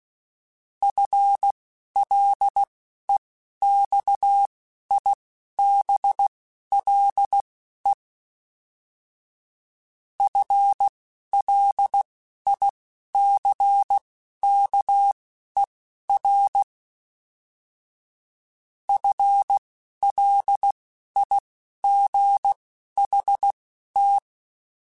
Morse code, heard as a keyed tone, about 16 wpm, spaced as for 11 wpm, 800 Hz. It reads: FLEXIBLE FLICKER FLIGHT